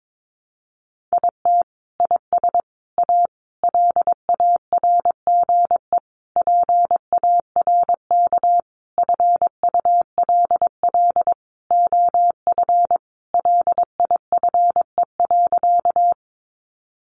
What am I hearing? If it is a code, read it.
Morse code, heard as a keyed tone, about 22 wpm, 700 Hz. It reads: IT IS A LARGE PARK FULL OF LIFE.
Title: Sentences from Top 1300 Words 22wpm